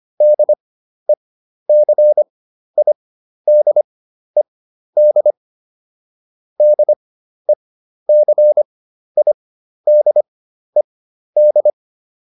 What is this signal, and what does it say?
Morse code, heard as a keyed tone, about 25 wpm, spaced as for 12 wpm, 600 Hz. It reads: DECIDED DECIDED